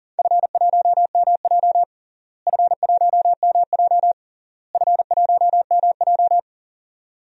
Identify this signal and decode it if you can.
Morse code, heard as a keyed tone, about 40 wpm, 700 Hz. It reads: F1MJ F1MJ F1MJ
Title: DXCC-100 Call Signs - 4 Characters - Repeated 3x - 40wpm